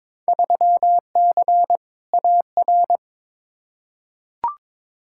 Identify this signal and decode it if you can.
Morse code, heard as a keyed tone, about 22 wpm, 700 Hz. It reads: 3C AR
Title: Field Day Exchanges Encore - 22wpm